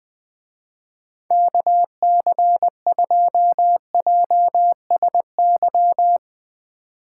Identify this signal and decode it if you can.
Morse code, heard as a keyed tone, about 21 wpm, 700 Hz. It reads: KC2JSY
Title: Speed Racing - USA Call Signs 17wpm